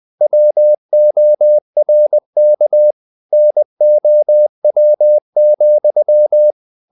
Morse code, heard as a keyed tone, about 20 wpm, 600 Hz. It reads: WORK NOW,
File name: the old man and the sea part 2 20wpm